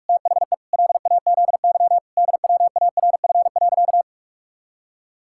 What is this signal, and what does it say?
Morse code, heard as a keyed tone, about 46 wpm, 700 Hz. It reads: THE LAZY DWARF.
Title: Pangram Challenge v2 - 45wpm